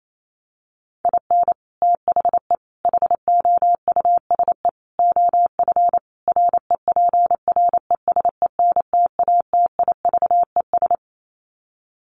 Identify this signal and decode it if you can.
Morse code, heard as a keyed tone, about 28 wpm, 700 Hz. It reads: IN THE HOUSE OF REPRESENTATIVES
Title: Sets of 5 Words Encore 28wpm